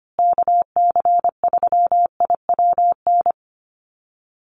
Morse code, heard as a keyed tone, about 25 wpm, 700 Hz. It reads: KC3IWN
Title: USA Call Signs Repeated 2x 25wpm